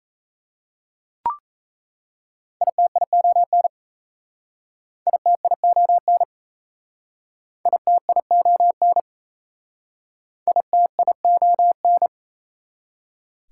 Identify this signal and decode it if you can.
Morse code, heard as a keyed tone, about 37 wpm, 690 Hz. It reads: ITION ITION ITION ITION